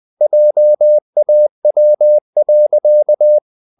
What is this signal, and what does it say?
Morse code, heard as a keyed tone, about 20 wpm, 600 Hz. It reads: JAW.
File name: the old man and the sea part 2 20wpm